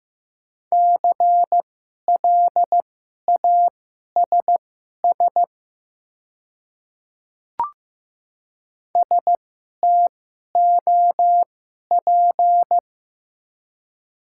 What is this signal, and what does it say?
Morse code, heard as a keyed tone, about 15 wpm, spaced as for 11 wpm, 700 Hz. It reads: CLASS STOP
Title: 2x Character Spacing - Top 500 Words - 15wpm